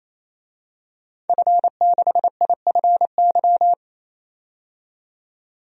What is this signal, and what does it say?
Morse code, heard as a keyed tone, about 29 wpm, 690 Hz. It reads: F6IFY
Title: DXCC-100 Call Signs - 5 Characters - 28wpm